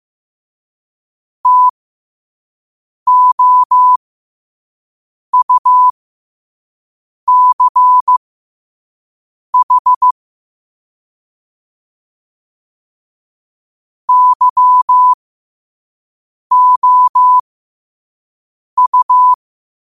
Morse code, heard as a keyed tone, about 15 wpm, spaced as for 5 wpm, 1000 Hz. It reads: TOUCH YOU
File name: qotdfiles / 5wpm